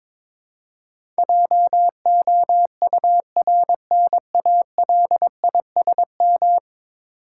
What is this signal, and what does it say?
Morse code, heard as a keyed tone, about 22 wpm, 700 Hz. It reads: JOURNALISM
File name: practice / 22wpm / 10 Letter Words - First 3 Letters Given - 22wpm